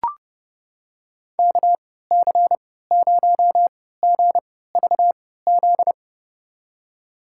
Morse code, heard as a keyed tone, about 30 wpm, spaced as for 17 wpm, 700 Hz. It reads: KC0GVZ
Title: USA Call Signs - 3x Character Spacing - 30wpm